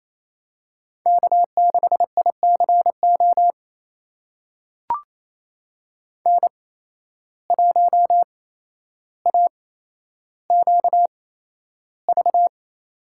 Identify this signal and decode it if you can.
Morse code, heard as a keyed tone, about 28 wpm, spaced as for 12 wpm, 700 Hz. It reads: K6ICO N1AQV
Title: USA Call Signs - 5 Characters - 8x Character Spacing - 28wpm